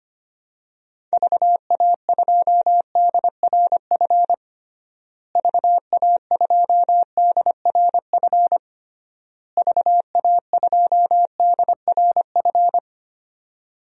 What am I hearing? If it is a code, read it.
Morse code, heard as a keyed tone, about 25 wpm, 700 Hz. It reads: VA2DRF VA2DRF VA2DRF
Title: DXCC-100 Call Signs - 6 Characters - Repeated 3x - 25wpm